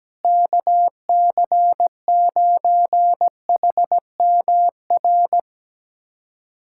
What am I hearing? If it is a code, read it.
Morse code, heard as a keyed tone, about 17 wpm, 700 Hz. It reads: KC9HMR
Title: USA Callsign DE Callsign 17wpm